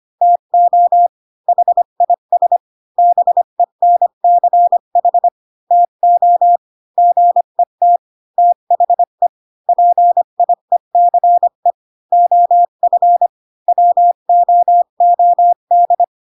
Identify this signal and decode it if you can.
Morse code, heard as a keyed tone, about 25 wpm, 700 Hz. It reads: TO HIS BENCH TO GET THE PIECE OF WOOD